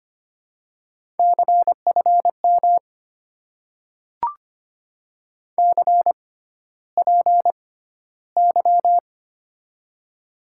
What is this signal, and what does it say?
Morse code, heard as a keyed tone, about 25 wpm, spaced as for 13 wpm, 700 Hz. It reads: CFM CPY
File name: Top 90 QSO Elements - 6x Character Spacing - 25wpm